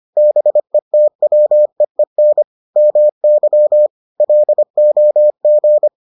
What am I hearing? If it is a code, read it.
Morse code, heard as a keyed tone, about 25 wpm, 600 Hz. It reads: BETWEEN MY LOG